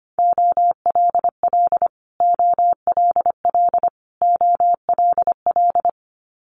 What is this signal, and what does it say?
Morse code, heard as a keyed tone, about 25 wpm, 700 Hz. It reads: OLL OLL OLL
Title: Multi-letter Phonemes Repeated 3x 25wpm